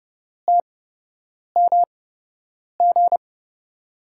Morse code, heard as a keyed tone, about 30 wpm, spaced as for 8 wpm, 700 Hz. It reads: TMG
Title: USA Call Signs - 8x Character Spacing - 30wpm